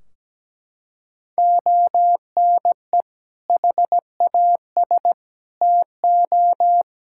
Morse code, heard as a keyed tone, about 17 wpm, 700 Hz. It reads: ONE HAS TO
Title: Sentences from Top 1600 Words 17wpm